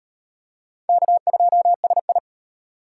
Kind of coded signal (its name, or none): Morse code